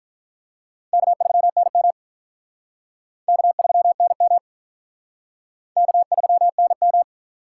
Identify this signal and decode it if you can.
Morse code, heard as a keyed tone, about 48 wpm, 700 Hz. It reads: K3NM K3NM K3NM